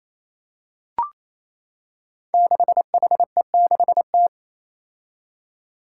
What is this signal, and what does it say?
Morse code, heard as a keyed tone, about 27 wpm, 700 Hz. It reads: 6HE6T
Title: Five Challenging Letter-Number Combinations 28wpm